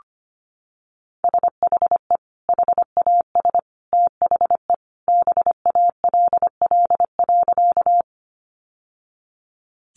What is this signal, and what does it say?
Morse code, heard as a keyed tone, about 25 wpm, 700 Hz. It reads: SHE HAS THE BALL.
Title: Sentences from Top 900 Words 25wpm